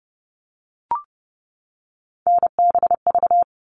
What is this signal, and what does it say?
Morse code, heard as a keyed tone, about 30 wpm, 700 Hz. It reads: NBV
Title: All Cut Numbers - 3-Digit 30wpm